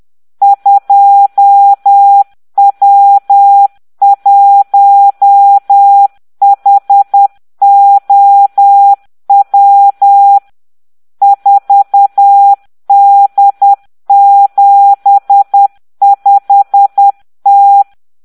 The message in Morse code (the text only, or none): 2W1HOW 4D75T